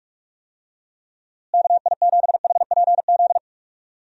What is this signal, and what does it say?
Morse code, heard as a keyed tone, about 44 wpm, 700 Hz. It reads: KI7HPZ
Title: USA Call Signs - 6 Characters - Repeated 2x - 45wpm